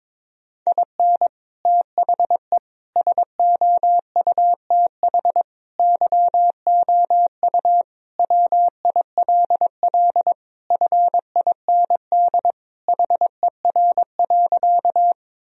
Morse code, heard as a keyed tone, about 22 wpm, 700 Hz. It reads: IN THE SOUTH YOU WILL FIND HER.
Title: Sentences from Top 1000 Words - Review Part 2 - 22wpm